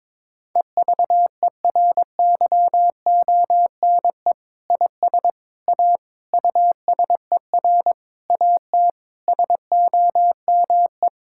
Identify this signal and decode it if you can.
Morse code, heard as a keyed tone, about 21 wpm, 700 Hz. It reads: EVERYONE IS A USER AT SOME
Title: Sentences from Top 1000 Words - Review Part 5 - 22wpm